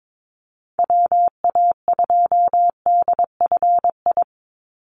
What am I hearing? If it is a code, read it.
Morse code, heard as a keyed tone, about 22 wpm, 700 Hz. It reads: WA2DFI
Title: Call Signs Repeated 2x - 22wpm